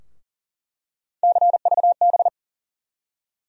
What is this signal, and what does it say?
Morse code, heard as a keyed tone, about 40 wpm, 700 Hz. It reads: CVB
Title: USA Call Sign Postfixes - 3 Letters 40wpm